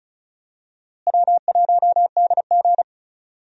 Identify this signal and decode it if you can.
Morse code, heard as a keyed tone, about 36 wpm, 700 Hz. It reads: W1DG
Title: USA Call Signs - 4 Characters - Repeated 3x - 35wpm